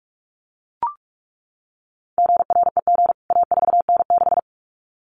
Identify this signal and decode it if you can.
Morse code, heard as a keyed tone, about 44 wpm, 700 Hz. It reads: GREG A4N6
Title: CWT Exchange 45wpm